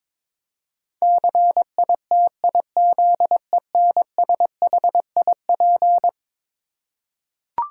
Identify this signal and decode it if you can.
Morse code, heard as a keyed tone, about 22 wpm, 700 Hz. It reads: CITIZENSHIP
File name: practice / 22wpm / Eleven Letter Words - 22wpm